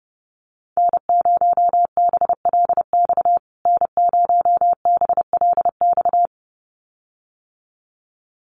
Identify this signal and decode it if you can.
Morse code, heard as a keyed tone, about 30 wpm, 700 Hz. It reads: N0BLX N0BLX
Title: USA Call Signs Repeated 2x 30wpm